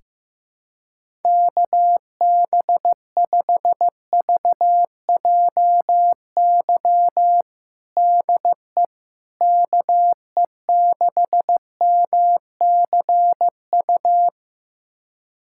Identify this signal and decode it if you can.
Morse code, heard as a keyed tone, about 15 wpm, 700 Hz. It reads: KB5VJY DE KE6MCU
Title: USA Callsign DE Callsign 15wpm